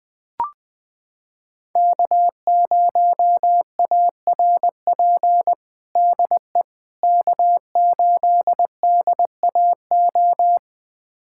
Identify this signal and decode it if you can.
Morse code, heard as a keyed tone, about 20 wpm, 700 Hz. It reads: K0ARP DE K8DAO
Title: USA Callsign DE Callsign - 5 Characters - 20wpm